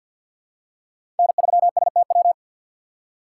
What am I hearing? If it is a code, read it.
Morse code, heard as a keyed tone, about 52 wpm, 690 Hz. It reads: N3STW
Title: USA Call Signs - 5 Characters - 6x Character Spacing - 50wpm